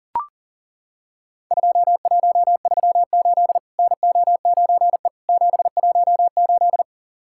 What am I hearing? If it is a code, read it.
Morse code, heard as a keyed tone, about 37 wpm, 700 Hz. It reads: 2138 NO9E 718
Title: Worked All Europe DX Contest - QTC Exchange - 40wpm